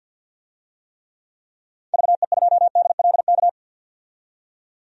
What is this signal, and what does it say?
Morse code, heard as a keyed tone, about 48 wpm, 700 Hz. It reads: VE2DLK
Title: DXCC-100 Call Signs - 6 Characters - 50wpm